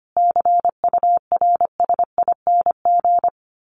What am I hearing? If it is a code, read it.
Morse code, heard as a keyed tone, about 25 wpm, 700 Hz. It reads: CURSING
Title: A Sound Of Thunder 25wpm